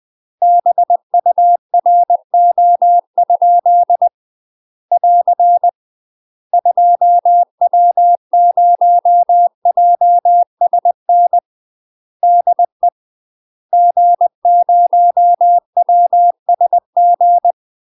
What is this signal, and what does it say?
Morse code, heard as a keyed tone, about 20 wpm, 700 Hz. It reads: BURO? <AR> 2W0JSN DE G0WSG